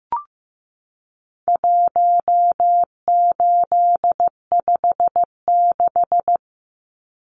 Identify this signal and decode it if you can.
Morse code, heard as a keyed tone, about 15 wpm, 700 Hz. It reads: 1856
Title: Numbers - 4-Digit 15wpm